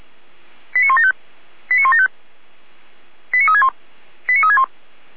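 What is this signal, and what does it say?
Five-tone selective call (ZVEI), heard as quick runs of five tones, tones of about 70 ms. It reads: 89175 89175 89361 89361
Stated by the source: ZVEI five-tone selective call received by SDR.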